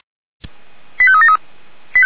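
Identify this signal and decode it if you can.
Five-tone selective call (ZVEI), heard as quick runs of five tones, tones of about 70 ms. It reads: 85373
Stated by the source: ZVEI five-tone call received on an SDR.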